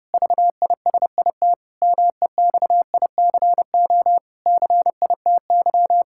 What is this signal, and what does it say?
Morse code, heard as a keyed tone, about 30 wpm, 700 Hz. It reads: VISIT MEXICO CITY
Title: Pangram Challenge v2 - 30wpm